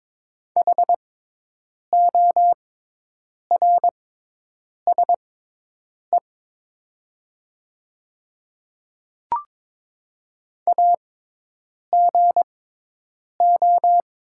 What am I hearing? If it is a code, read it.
Morse code, heard as a keyed tone, about 22 wpm, spaced as for 8 wpm, 700 Hz. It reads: HORSE AGO
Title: 6x Character Spacing - Top 500 Words - 22wpm